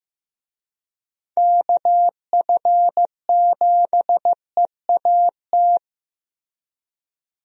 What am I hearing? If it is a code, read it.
Morse code, heard as a keyed tone, about 15 wpm, 700 Hz. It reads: KF7EAT
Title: USA Call Signs - 6 Characters - Repeated 3x - 15wpm